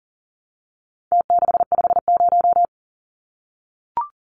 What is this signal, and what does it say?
Morse code, heard as a keyed tone, about 40 wpm, 700 Hz. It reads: T650
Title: Numbers - Contest Style - 4-Digit 40wpm